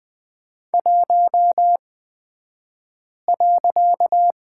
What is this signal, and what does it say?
Morse code, heard as a keyed tone, about 20 wpm, spaced as for 5 wpm, 700 Hz. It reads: 1.
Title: 254 Lesson 41d - Period - Two Characters 20wpm